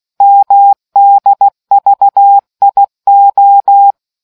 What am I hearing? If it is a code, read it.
Morse code, heard as a keyed tone, about 16 wpm, 800 Hz. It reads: MDVIO